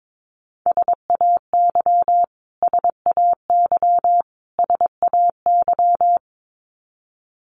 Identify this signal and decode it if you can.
Morse code, heard as a keyed tone, about 22 wpm, 700 Hz. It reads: SAY SAY SAY
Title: Top 200 Words Repeated 22wpm